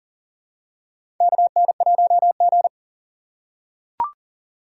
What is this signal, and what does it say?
Morse code, heard as a keyed tone, about 37 wpm, 700 Hz. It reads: KN1G
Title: USA Call Signs - 4 Characters - 2x Character Spacing - 40wpm